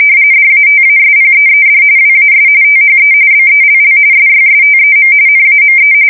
RTTY, about 45.45 baud. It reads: DATASPEED WAS THE BELL SYSTEM NAME
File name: fsk170 / teletype-corp